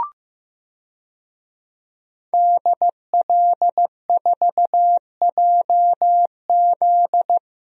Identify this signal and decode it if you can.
Morse code, heard as a keyed tone, about 15 wpm, 700 Hz. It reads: DL4JZ
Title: DXCC-100 Call Signs - 5 Characters - Repeated 3x - 15wpm